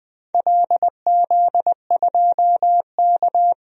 Morse code, heard as a keyed tone, about 20 wpm, 700 Hz. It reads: LZ2K
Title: DXCC-100 Call Signs - 4 Characters - 20wpm